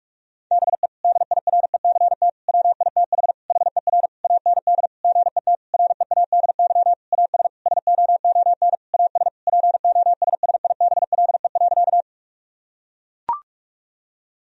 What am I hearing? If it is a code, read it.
Morse code, heard as a keyed tone, about 42 wpm, 700 Hz. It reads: BE DIRECT WITH HER AND GET READY AS SOON AS POSSIBLE.
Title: Sentences from Top 1000 Words - Review Part 2 - 45wpm